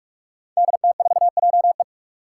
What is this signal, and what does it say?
Morse code, heard as a keyed tone, about 44 wpm, 700 Hz. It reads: DT4JE
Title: Five Challenging Letter-Number Combinations 45wpm